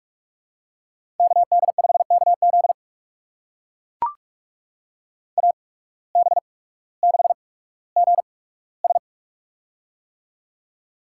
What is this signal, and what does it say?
Morse code, heard as a keyed tone, about 44 wpm, spaced as for 19 wpm, 700 Hz. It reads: KD5KZ AB6GS
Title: USA Call Signs - 5 Characters - 8x Character Spacing - 45wpm